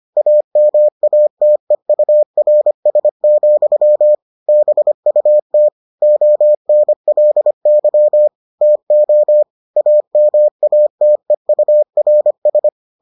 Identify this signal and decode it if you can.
Morse code, heard as a keyed tone, about 25 wpm, 600 Hz. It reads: AMATEURS, BUT ONLY TO AMATEURS